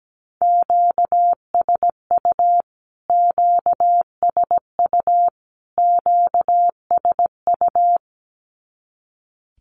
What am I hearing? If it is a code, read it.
Morse code, heard as a keyed tone, about 17 wpm, 700 Hz. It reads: QSU QSU QSU
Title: Q-Codes Repeated 3x 17wpm